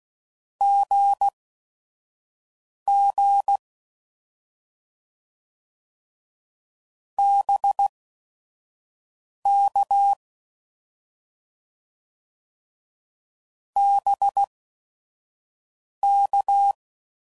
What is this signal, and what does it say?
Morse code, heard as a keyed tone, about 16 wpm, spaced as for 5 wpm, 800 Hz. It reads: GG BK BK
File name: Mixed / Abreviations / Abreviations